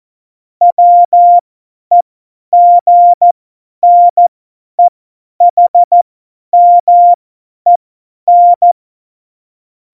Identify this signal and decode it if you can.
Morse code, heard as a keyed tone, about 14 wpm, spaced as for 10 wpm, 700 Hz. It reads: WEGNEHMEN
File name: Morsekurs / Witze / Witz 11. September 2018, 50 F70